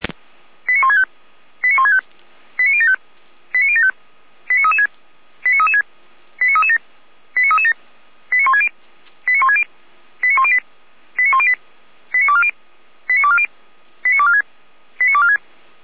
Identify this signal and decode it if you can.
Five-tone selective call (ZVEI), heard as quick runs of five tones, tones of about 80 ms. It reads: 89165 89165 89075 89075 89336 89336 89337 89337 89160 89160 89108 89108 89240 89240 89246 89246
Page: /455/SDR/ZVEI/89170/